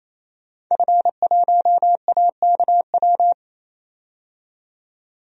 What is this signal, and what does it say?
Morse code, heard as a keyed tone, about 28 wpm, 700 Hz. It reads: F1AKW
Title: DXCC-100 Call Signs - 5 Characters - 28wpm